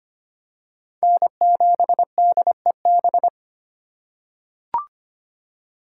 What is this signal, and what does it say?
Morse code, heard as a keyed tone, about 25 wpm, 700 Hz. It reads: N7DEB
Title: USA Call Signs - 5 Characters - 4x Character Spacing - 25wpm